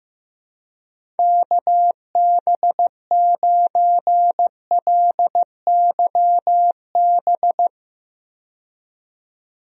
Morse code, heard as a keyed tone, about 15 wpm, 700 Hz. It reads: KB9LYB
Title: USA Call Signs - 6 Characters - 15wpm